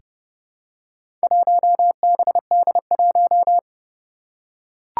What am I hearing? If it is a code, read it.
Morse code, heard as a keyed tone, about 30 wpm, 700 Hz. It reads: 1BD1
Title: Four Challenging Letter-Number Combinations 30wpm